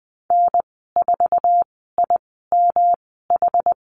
Morse code, heard as a keyed tone, about 20 wpm, spaced as for 14 wpm, 700 Hz. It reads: N4IMH
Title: 128 Lesson 21g - Number 6 - Call Signs 20wpm